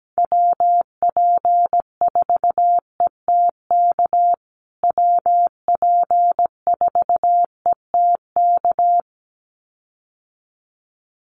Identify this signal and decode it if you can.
Morse code, heard as a keyed tone, about 17 wpm, 700 Hz. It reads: WP4ETK WP4ETK
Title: USA Call Signs Repeated 2x 17wpm